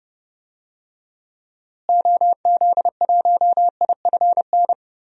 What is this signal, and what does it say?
Morse code, heard as a keyed tone, about 30 wpm, 700 Hz. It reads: OZ1IFN
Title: DXCC-100 Call Signs - 6 Characters - Repeated 3x - 30wpm